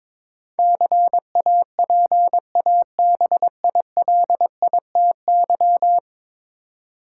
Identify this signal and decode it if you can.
Morse code, heard as a keyed tone, about 22 wpm, 700 Hz. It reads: CAPABILITY
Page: morse-code-ninja-practice-22wpm